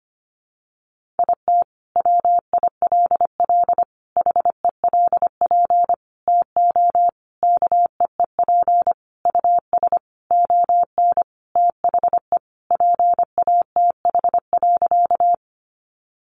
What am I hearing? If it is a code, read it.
Morse code, heard as a keyed tone, about 25 wpm, 700 Hz. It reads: IT WILL HELP TO KEEP US ON THE PATH.
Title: Sentences from Top 1400 Words 25wpm